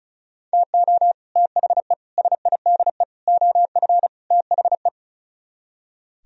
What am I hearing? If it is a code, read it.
Morse code, heard as a keyed tone, about 43 wpm, 700 Hz. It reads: TO THE SIDE OF THE
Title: Speed-Racing - Sets of 5 Words 35wpm